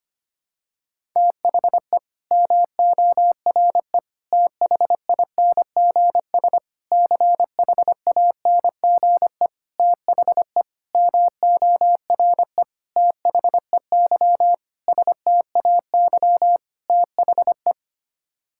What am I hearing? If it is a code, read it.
Morse code, heard as a keyed tone, about 25 wpm, 700 Hz. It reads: THE MORE THINGS CHANGE THE MORE THEY STAY THE
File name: Sets of 9 Words - 25wpm